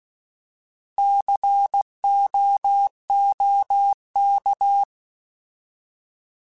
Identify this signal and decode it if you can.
Morse code, heard as a keyed tone, about 16 wpm, 800 Hz. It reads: COOK
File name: Alphabetical / Alphabetical mp3s / Cs @15